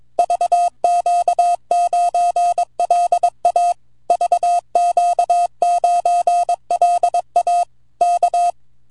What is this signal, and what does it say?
Morse code, heard as a keyed tone, about 22 wpm, 700 Hz. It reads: VQ9LA VQ9LA K
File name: vq9la / ok1fdy 10AUG08 0115Z 160mtr